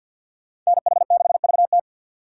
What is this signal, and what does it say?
Morse code, heard as a keyed tone, about 61 wpm, 700 Hz. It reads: NH6VT
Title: Speed Racing - USA 2x2 Call Signs 50wpm